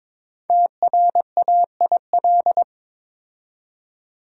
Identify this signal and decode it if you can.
Morse code, heard as a keyed tone, about 22 wpm, 700 Hz. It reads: TRAIL
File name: 5 Letter Words - First 3 Letters Given - 22wpm